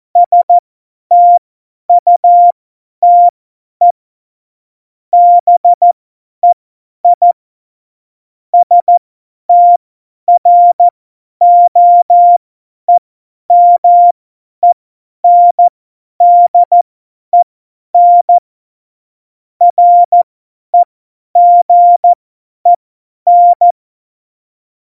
Morse code, heard as a keyed tone, about 14 wpm, spaced as for 10 wpm, 700 Hz. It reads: STUTE BEI STROEMENDEN REGEN
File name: Morsekurs / Witze / Witz 28. August 2018, 50 F70